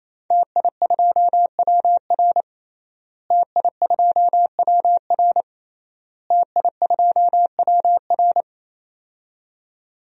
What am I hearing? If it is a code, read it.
Morse code, heard as a keyed tone, about 28 wpm, 700 Hz. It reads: TI2WR TI2WR TI2WR
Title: DXCC-100 Call Signs - 5 Characters - Repeated 3x - 28wpm